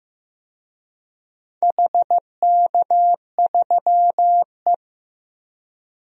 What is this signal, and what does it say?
Morse code, heard as a keyed tone, about 15 wpm, 700 Hz. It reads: HK3E
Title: DXCC-100 Call Signs - 4 Characters - 15wpm